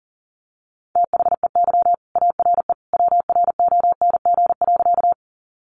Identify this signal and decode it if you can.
Morse code, heard as a keyed tone, about 40 wpm, 700 Hz. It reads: THEY ARE WRONG.
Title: Sentences from Top 900 Words 40wpm